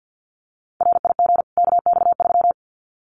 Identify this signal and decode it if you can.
Morse code, heard as a keyed tone, about 50 wpm, 700 Hz. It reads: RIG KX3
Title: Common QSO Phrases 50wpm